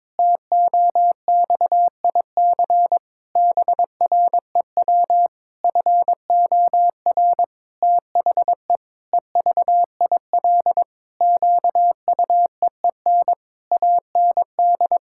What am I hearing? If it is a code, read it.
Morse code, heard as a keyed tone, about 22 wpm, 700 Hz. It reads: TOXIC BREW FOR THE EVIL QUEEN AND
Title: Pangram Challenge v2 - 22wpm